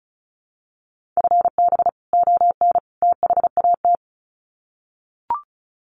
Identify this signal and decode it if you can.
Morse code, heard as a keyed tone, about 34 wpm, 700 Hz. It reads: FB ON THAT
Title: Common QSO Phrases 35wpm